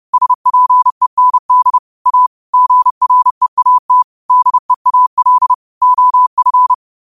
Morse code, heard as a keyed tone, about 30 wpm, 1000 Hz. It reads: SPEND A GREAT DEAL OF
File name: qotdfiles / 30wpm